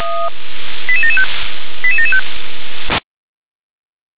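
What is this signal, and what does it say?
Five-tone selective call (ZVEI), heard as quick runs of five tones, tones of about 70 ms. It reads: 88775 88775
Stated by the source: ZVEI five-tone selective call received by SDR.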